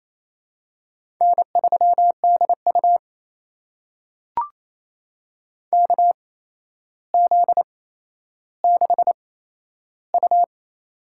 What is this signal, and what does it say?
Morse code, heard as a keyed tone, about 28 wpm, spaced as for 12 wpm, 700 Hz. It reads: N3DU KZ6U
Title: USA Call Signs - 4 Characters - 8x Character Spacing - 28wpm